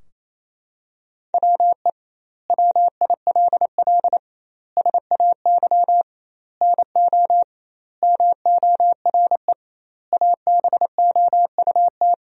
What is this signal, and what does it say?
Morse code, heard as a keyed tone, about 28 wpm, 700 Hz. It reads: WE WILL SAY NO MORE ABOUT
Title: 2x Word-Spacing - Sentences from Top 200 Words - 28wpm